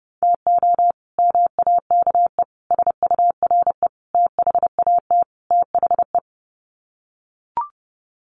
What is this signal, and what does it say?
Morse code, heard as a keyed tone, about 30 wpm, 700 Hz. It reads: TO MAKE SURE THAT THE
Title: Sets of 5 Words 30wpm